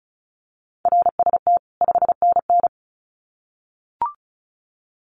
Morse code, heard as a keyed tone, about 34 wpm, 700 Hz. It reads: RST 5NN
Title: Common QSO Phrases 35wpm